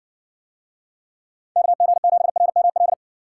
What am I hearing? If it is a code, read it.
Morse code, heard as a keyed tone, about 60 wpm, 700 Hz. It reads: KG7RML